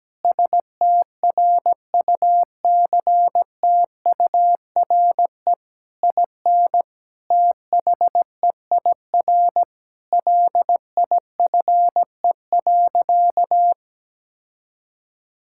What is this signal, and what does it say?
Morse code, heard as a keyed tone, about 17 wpm, 700 Hz. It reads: STRUCTURE IN THEIR LIFE.
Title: Sentences from Top 1000 Words - Review Part 6 - 17wpm